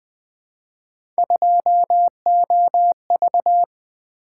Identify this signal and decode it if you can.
Morse code, heard as a keyed tone, about 20 wpm, 700 Hz. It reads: 2OV
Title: Three Letter-Number Combinations - 20wpm